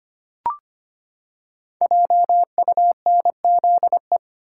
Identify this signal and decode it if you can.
Morse code, heard as a keyed tone, about 25 wpm, 700 Hz. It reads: JUNZE
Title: Code Groups - 5 Letters - 25wpm